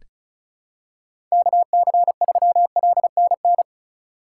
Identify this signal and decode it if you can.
Morse code, heard as a keyed tone, about 36 wpm, 700 Hz. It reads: KC3LNN
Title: USA Call Signs - 6 Characters - 2x Character Spacing - 35wpm